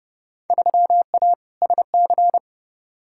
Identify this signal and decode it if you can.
Morse code, heard as a keyed tone, about 30 wpm, 700 Hz. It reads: 3A SC
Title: Field Day Exchanges - 30wpm